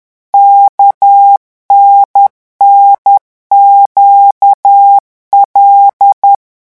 Morse code, heard as a keyed tone, about 10 wpm, 800 Hz. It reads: KNNQL
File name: morse mp3 / revision11 lettres vitesse 10